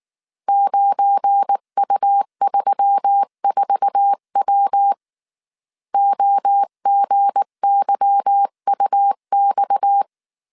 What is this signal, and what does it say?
Morse code, heard as a keyed tone, about 19 wpm, 800 Hz. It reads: 9U34W OGYUX
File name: CW / DET / 1 9 1500C S B 18wpm 2551